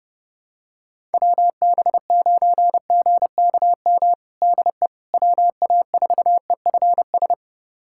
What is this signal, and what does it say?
Morse code, heard as a keyed tone, about 30 wpm, 700 Hz. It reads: WB9GKM DE WA4EFS
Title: USA Callsign DE Callsign - 6 Characters - 30wpm